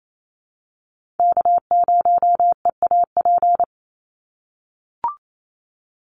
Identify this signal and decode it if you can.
Morse code, heard as a keyed tone, about 28 wpm, 700 Hz. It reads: K0EAP
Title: NA SOTA Chaser Callsigns 28wpm